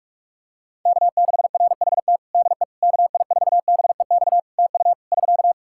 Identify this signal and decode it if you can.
Morse code, heard as a keyed tone, about 46 wpm, 700 Hz. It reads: K6RHT DE KI4BEX TU <SK>
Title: K1USN SST QSO Practice 45wpm